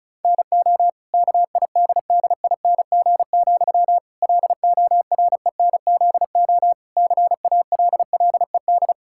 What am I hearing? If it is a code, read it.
Morse code, heard as a keyed tone, about 34 wpm, 700 Hz. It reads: NO KIDDING, LORENZO CALLED
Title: Pangram Challenge v2 - 35wpm